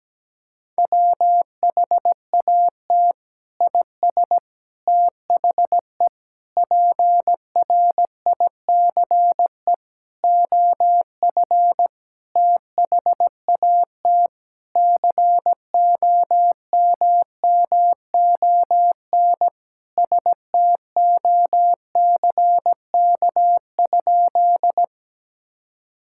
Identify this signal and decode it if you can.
Morse code, heard as a keyed tone, about 17 wpm, 700 Hz. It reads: WHAT IS THE PRICE OF THAT COMMON STOCK?
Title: Sentences from Top 1000 Words - Review Part 5 - 17wpm